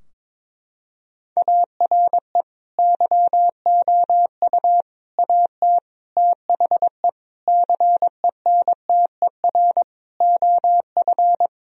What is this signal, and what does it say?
Morse code, heard as a keyed tone, about 22 wpm, 700 Hz. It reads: ARE YOU AT THE CENTER OF
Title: Sentences from Top 1000 Words - Review Part 2 - 22wpm